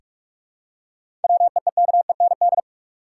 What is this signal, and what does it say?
Morse code, heard as a keyed tone, about 46 wpm, 700 Hz. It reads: WEEKEND
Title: Seven Letter Words - 45wpm